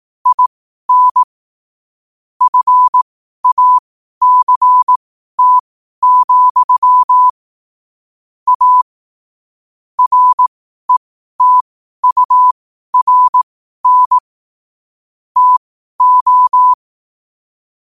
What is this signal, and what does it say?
Morse code, heard as a keyed tone, about 18 wpm, spaced as for 12 wpm, 1000 Hz. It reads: IN FACT, A RETURN TO
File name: qotdfiles / 12wpm